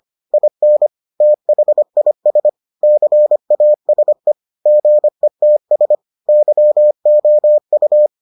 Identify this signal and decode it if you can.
Morse code, heard as a keyed tone, about 25 wpm, 600 Hz. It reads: IN THIS CASE GETS YOU